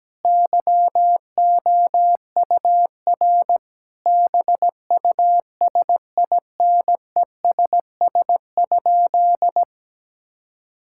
Morse code, heard as a keyed tone, about 17 wpm, 700 Hz. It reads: YOUR BUSINESS?
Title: Sentences from Top 1000 Words - Review Part 5 - 17wpm